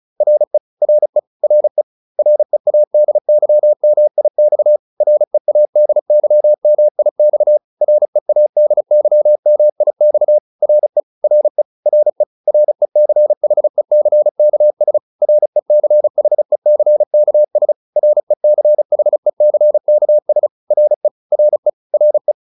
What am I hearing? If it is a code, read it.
Morse code, heard as a keyed tone, about 34 wpm, 600 Hz. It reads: RE RE RE READYMIX READYMIX READYMIX RE RE RE RECHECKS RECHECKS RECHECKS RE RE RE